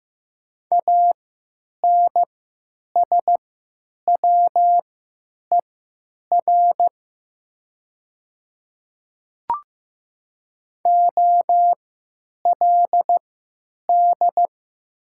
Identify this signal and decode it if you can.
Morse code, heard as a keyed tone, about 15 wpm, spaced as for 9 wpm, 700 Hz. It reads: ANSWER OLD
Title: 3x Character Spacing - Top 300 Words - 15wpm